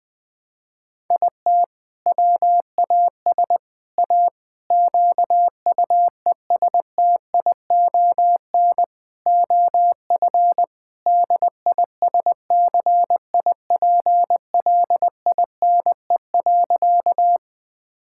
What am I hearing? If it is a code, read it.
Morse code, heard as a keyed tone, about 20 wpm, 700 Hz. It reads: IT WAS A QUESTION OF DISCIPLINE.
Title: Sentences from Top 1000 Words - Review Part 6 - 20wpm